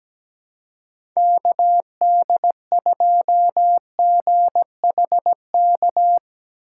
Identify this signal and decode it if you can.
Morse code, heard as a keyed tone, about 17 wpm, 700 Hz. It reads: KD2GHK
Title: USA Call Signs - 2x Character Spacing - 17wpm